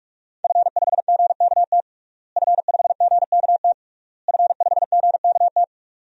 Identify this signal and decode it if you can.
Morse code, heard as a keyed tone, about 43 wpm, 700 Hz. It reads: F5GKT F5GKT F5GKT
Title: DXCC-100 Call Signs - 5 Characters - Repeated 3x - 45wpm